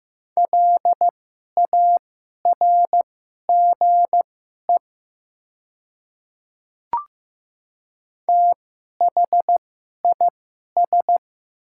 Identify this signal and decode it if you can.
Morse code, heard as a keyed tone, about 15 wpm, spaced as for 11 wpm, 700 Hz. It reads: LARGE THIS
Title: 2x Character Spacing - Top 300 Words - 15wpm